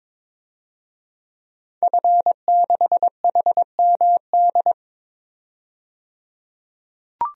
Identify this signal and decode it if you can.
Morse code, heard as a keyed tone, about 22 wpm, 700 Hz. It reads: F6HMD
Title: DXCC-100 Call Signs - 5 Characters - 22wpm